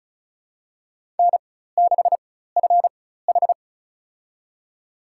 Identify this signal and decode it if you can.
Morse code, heard as a keyed tone, about 36 wpm, spaced as for 16 wpm, 700 Hz. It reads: N6FH
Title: USA Call Signs - 4 Characters - 4x Character Spacing - 35wpm